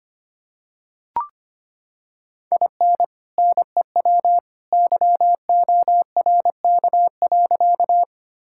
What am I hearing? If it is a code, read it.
Morse code, heard as a keyed tone, about 25 wpm, 700 Hz. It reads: IN NEW YORK.
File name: practice / 25wpm / Sentences with 2-Word Phrases - 25wpm